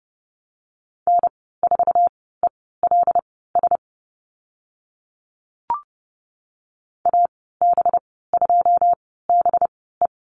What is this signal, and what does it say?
Morse code, heard as a keyed tone, about 30 wpm, spaced as for 17 wpm, 700 Hz. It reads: N4ELS AB2BE
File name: Lesson 37g - Letter x - Call Signs 30wpm